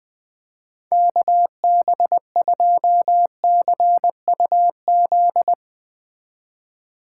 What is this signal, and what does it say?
Morse code, heard as a keyed tone, about 21 wpm, 700 Hz. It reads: KB2CUZ